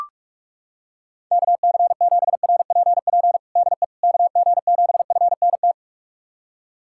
Morse code, heard as a keyed tone, about 44 wpm, 700 Hz. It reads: KC7RPP DE KG7FNT
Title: USA Callsign DE Callsign - 6 Characters - 45wpm